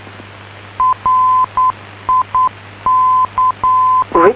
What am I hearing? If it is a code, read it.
Morse code, heard as a keyed tone, about 9 wpm, 1020 Hz. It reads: RIC